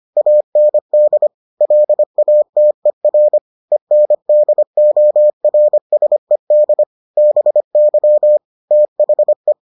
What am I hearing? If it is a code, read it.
Morse code, heard as a keyed tone, about 25 wpm, 600 Hz. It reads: AND LATER ENDORSED BY THE